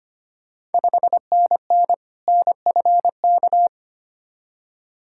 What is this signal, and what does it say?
Morse code, heard as a keyed tone, about 25 wpm, 700 Hz. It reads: 5NN NFK
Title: Ontario QSO Party Exchange 25wpm